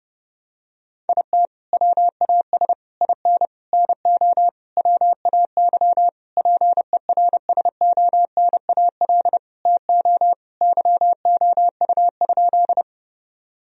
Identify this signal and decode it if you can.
Morse code, heard as a keyed tone, about 30 wpm, 700 Hz. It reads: IT WAS IN NO WAY PERSONAL TO YOU?
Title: Sentences from Top 1000 Words - Review Part 4 - 30wpm